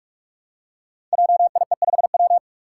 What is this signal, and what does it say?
Morse code, heard as a keyed tone, about 44 wpm, 700 Hz. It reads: JIE5W
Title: Code Groups - 5 Letter:Numbers - 45wpm